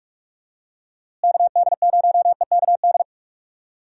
Morse code, heard as a keyed tone, about 46 wpm, 700 Hz. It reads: KD0EKD
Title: USA Call Signs - 6 Characters - Repeated 3x - 45wpm